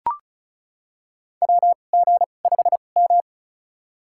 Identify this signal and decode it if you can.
Morse code, heard as a keyed tone, about 34 wpm, spaced as for 25 wpm, 700 Hz. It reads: WG5M
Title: USA Call Signs - 2x Character Spacing - 35wpm